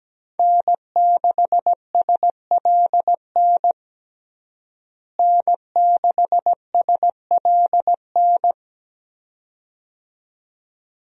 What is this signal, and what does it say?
Morse code, heard as a keyed tone, about 17 wpm, 700 Hz. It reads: N6SLN N6SLN
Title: USA Call Signs - 5 Characters - Repeated 2x - 17wpm